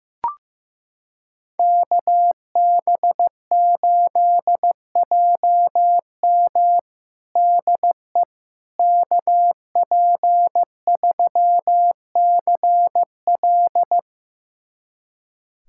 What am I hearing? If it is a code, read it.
Morse code, heard as a keyed tone, about 15 wpm, 700 Hz. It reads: KB8JM DE KP3CL
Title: USA Callsign DE Callsign - 5 Characters - 15wpm